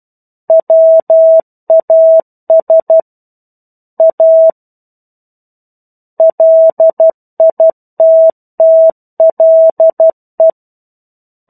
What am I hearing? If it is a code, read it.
Morse code, heard as a keyed tone, about 12 wpm, 650 Hz. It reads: WAS A LITTLE
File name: Chapter-01 12 WPM